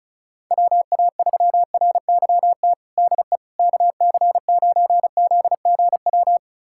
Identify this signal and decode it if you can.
Morse code, heard as a keyed tone, about 36 wpm, 700 Hz. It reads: WA3RYT DE KC9ZGW
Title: USA Callsign DE Callsign 35wpm